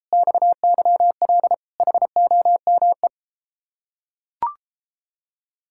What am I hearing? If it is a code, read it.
Morse code, heard as a keyed tone, about 33 wpm, 700 Hz. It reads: XYL HOME